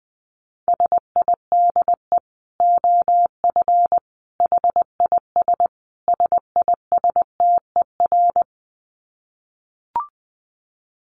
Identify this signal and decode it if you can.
Morse code, heard as a keyed tone, about 19 wpm, 700 Hz. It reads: SIDE OF HIS SISTER
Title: Dit Challenge 20wpm